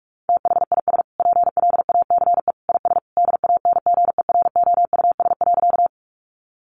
Morse code, heard as a keyed tone, about 44 wpm, 700 Hz. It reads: THIS PLACE IS DANGEROUS.
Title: Sentences from Top 1200 Words 45wpm